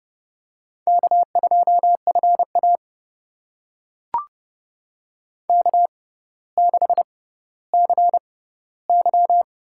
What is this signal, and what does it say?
Morse code, heard as a keyed tone, about 30 wpm, spaced as for 15 wpm, 700 Hz. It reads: K2FA K6CY
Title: USA Call Signs - 4 Characters - 6x Character Spacing - 30wpm